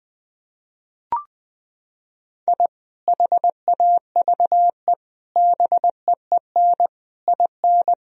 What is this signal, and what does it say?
Morse code, heard as a keyed tone, about 20 wpm, 700 Hz. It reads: I HAVE BEEN IN
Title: Sentences from Top 1000 Words - Review Part 3 - 20wpm